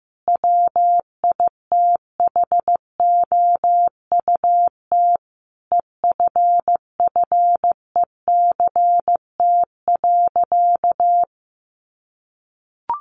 Morse code, heard as a keyed tone, about 15 wpm, 700 Hz. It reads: WITHOUT EFFECT.
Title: Sentences from Top 1400 Words 15wpm